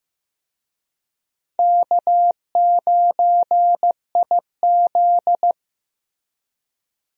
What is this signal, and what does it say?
Morse code, heard as a keyed tone, about 16 wpm, 700 Hz. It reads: K9IZ
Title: USA Call Signs - 4 Characters - Repeated 2x - 15wpm